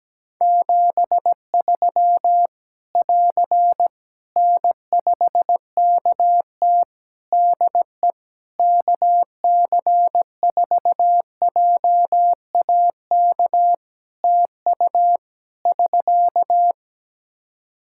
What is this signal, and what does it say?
Morse code, heard as a keyed tone, about 17 wpm, 700 Hz. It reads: 73 <AR> N5KT DE KC4JAK TU <SK>
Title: K1USN SST QSO Practice 17wpm